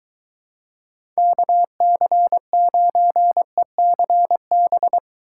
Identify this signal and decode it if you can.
Morse code, heard as a keyed tone, about 23 wpm, 700 Hz. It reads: KC9ECB